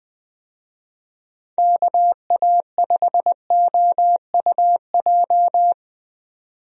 Morse code, heard as a keyed tone, about 20 wpm, 700 Hz. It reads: KA5OUJ
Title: USA Call Signs - 6 Characters - 20wpm